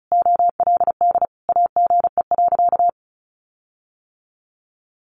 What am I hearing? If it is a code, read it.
Morse code, heard as a keyed tone, about 34 wpm, 700 Hz. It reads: OLD AGE.
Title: Sentences from Top 1300 Words 35wpm